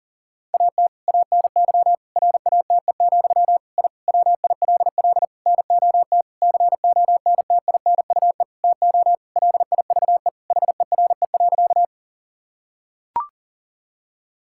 Morse code, heard as a keyed tone, about 40 wpm, 700 Hz. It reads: AT ANY RATE, I WILL NOT CONTINUE TO LIVE HERE.
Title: Sentences from Top 1000 Words - Review Part 6 - 40wpm